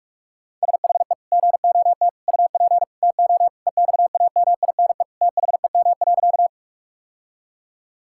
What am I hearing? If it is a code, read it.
Morse code, heard as a keyed tone, about 42 wpm, 700 Hz. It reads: SHE GOT UP TO EXAMINE THEM.
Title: Sentences from Top 1600 Words 45wpm